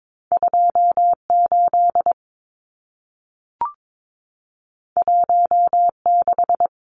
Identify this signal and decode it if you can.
Morse code, heard as a keyed tone, about 22 wpm, 700 Hz. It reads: 28 16
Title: Numbers - 2-Digit 22wpm